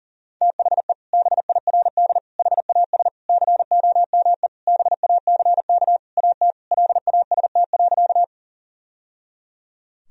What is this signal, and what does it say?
Morse code, heard as a keyed tone, about 37 wpm, 700 Hz. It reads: THE BIRD HAS COME BACK AT LAST.
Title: Sentences from Top 1000 Words - Review Part 2 - 40wpm